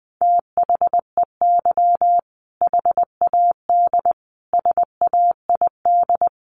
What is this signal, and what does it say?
Morse code, heard as a keyed tone, about 20 wpm, 700 Hz. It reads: THEY HAD SAID
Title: Sentences from Top 400 Words 20wpm